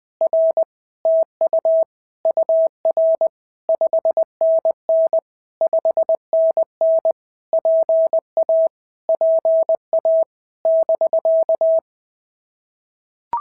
Morse code, heard as a keyed tone, about 20 wpm, 650 Hz. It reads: R TU UR 5NN 5NN PA PA <BK>
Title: POTA QSO Practice - 20wpm